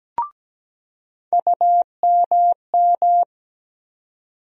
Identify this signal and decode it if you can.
Morse code, heard as a keyed tone, about 17 wpm, 700 Hz. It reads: UMM
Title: USA Call Sign Postfixes - 3 Letters 17wpm